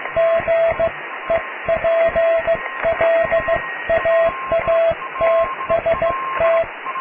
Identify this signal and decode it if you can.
Morse code, heard as a keyed tone, about 15 wpm, 660 Hz, with radio noise. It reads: GEPLAATST
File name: cw / tekst(a)+QRM